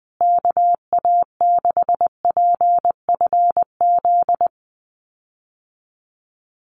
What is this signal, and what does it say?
Morse code, heard as a keyed tone, about 20 wpm, 700 Hz. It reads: KA6PFZ
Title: USA Call Signs Repeated 2x 20wpm